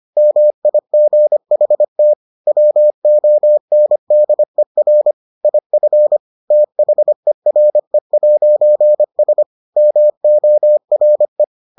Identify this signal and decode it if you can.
Morse code, heard as a keyed tone, about 25 wpm, 600 Hz. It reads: MIGHT WONDER IF THERE'S MORE